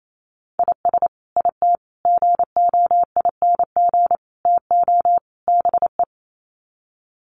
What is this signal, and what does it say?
Morse code, heard as a keyed tone, about 28 wpm, 700 Hz. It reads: IS IT GOING TO BE
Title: Sets of 5 Words Encore2 28wpm